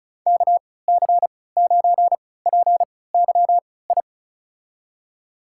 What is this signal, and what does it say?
Morse code, heard as a keyed tone, about 34 wpm, spaced as for 20 wpm, 700 Hz. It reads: KC9PYI
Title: USA Call Signs - 6 Characters - 3x Character Spacing - 35wpm